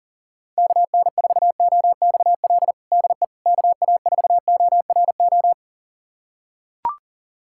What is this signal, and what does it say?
Morse code, heard as a keyed tone, about 40 wpm, 700 Hz. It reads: KN4OXL DE KA4ORO
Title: USA Callsign DE Callsign 40wpm